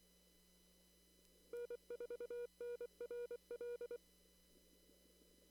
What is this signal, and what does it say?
Morse code, heard as a keyed tone, about 23 wpm, 1440 Hz. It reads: N4NRL
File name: CW / kt5j wpx cw-op-k5pi